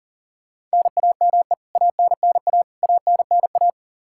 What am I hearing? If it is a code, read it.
Morse code, heard as a keyed tone, about 37 wpm, 700 Hz. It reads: NAME ANNA ANNA
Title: Practice QSO - Exchange 1 - Beginning - 40wpm